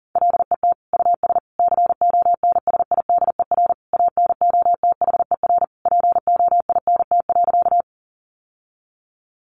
Morse code, heard as a keyed tone, about 40 wpm, 700 Hz. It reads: LET US CONSIDER ANOTHER POINT.
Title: Sentences from Top 600 Words 40wpm